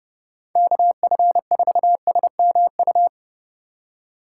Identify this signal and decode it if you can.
Morse code, heard as a keyed tone, about 38 wpm, 700 Hz. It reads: KF4SMU